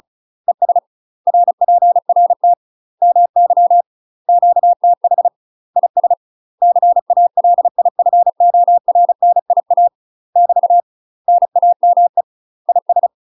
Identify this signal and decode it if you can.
Morse code, heard as a keyed tone, about 35 wpm, 700 Hz. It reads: ES RPRT MY QTH IS CALIFORNIA <BT> NAME IS
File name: UK MASTER QSOs TRAINING / 35WPM / UK MASTER QSOs 0026